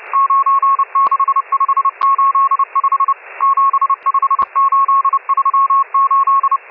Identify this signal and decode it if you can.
Morse code, heard as a keyed tone, about 30 wpm, 1100 Hz, with radio noise. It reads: 96585 75838
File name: M12- -1010utc